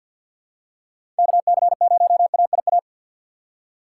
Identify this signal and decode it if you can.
Morse code, heard as a keyed tone, about 53 wpm, 700 Hz. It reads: KC0AIA
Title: USA Call Signs - 2x Character Spacing - 50wpm